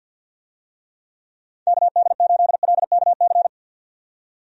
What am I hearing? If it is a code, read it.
Morse code, heard as a keyed tone, about 51 wpm, 700 Hz. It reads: KD8LKC
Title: USA Call Signs - 6 Characters - 50wpm